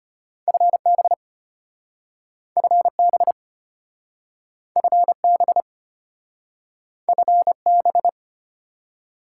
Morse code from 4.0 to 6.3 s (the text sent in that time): FB